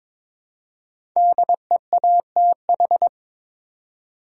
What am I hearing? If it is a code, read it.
Morse code, heard as a keyed tone, about 22 wpm, 700 Hz. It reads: DEATH
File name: 5 Letter Words - First 3 Letters Given - 22wpm